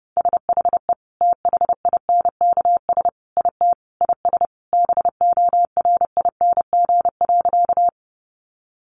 Morse code, heard as a keyed tone, about 28 wpm, 700 Hz. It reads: SHE THINKS IT IS BORING.
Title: Easy Phrases 30wpm